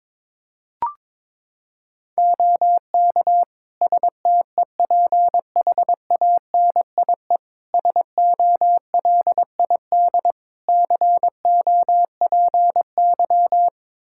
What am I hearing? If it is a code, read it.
Morse code, heard as a keyed tone, about 22 wpm, 700 Hz. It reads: OK STEPHANIE SOLID COPY
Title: Practice QSO - Exchange 3 - More Info - 22wpm